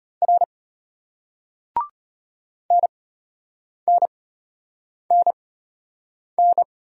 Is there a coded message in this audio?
Yes: Morse code from 0.2 to 6.6 s, heard as a keyed tone, about 33 wpm, spaced as for 8 wpm, 700 Hz. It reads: R NNNN